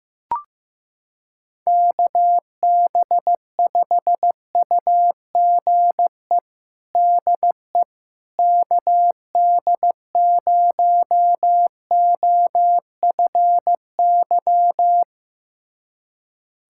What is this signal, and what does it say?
Morse code, heard as a keyed tone, about 15 wpm, 700 Hz. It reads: KB5UGE DE KD0OFY
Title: USA Callsign DE Callsign 15wpm